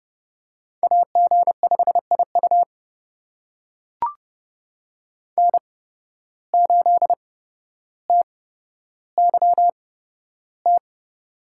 Morse code, heard as a keyed tone, about 31 wpm, spaced as for 13 wpm, 700 Hz. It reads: AG5IU N8TYT
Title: USA Call Signs - 5 Characters - 8x Character Spacing - 30wpm